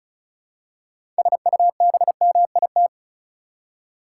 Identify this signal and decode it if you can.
Morse code, heard as a keyed tone, about 34 wpm, 700 Hz. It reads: SUBMIT
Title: Six Letter Words - 35wpm